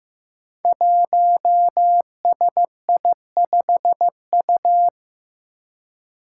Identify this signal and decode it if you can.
Morse code, heard as a keyed tone, about 15 wpm, 700 Hz. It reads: 1SI5U
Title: Five Challenging Letter-Number Combinations 15wpm